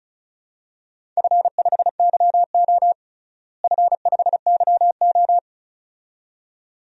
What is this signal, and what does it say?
Morse code, heard as a keyed tone, about 34 wpm, 700 Hz. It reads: F5YO F5YO
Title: DXCC-100 Call Signs - 4 Characters - Repeated 2x - 35wpm